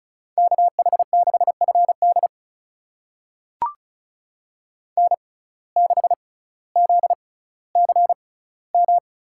Morse code, heard as a keyed tone, about 36 wpm, spaced as for 18 wpm, 700 Hz. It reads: KH6FD N6ZCM
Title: USA Call Signs - 5 Characters - 6x Character Spacing - 35wpm